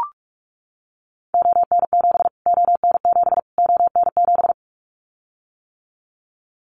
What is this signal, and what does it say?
Morse code, heard as a keyed tone, about 44 wpm, 700 Hz. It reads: ON7 ON7 ON7
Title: Callsign Prefixes for Top 25 DXCC - Repeated 3x - 45wpm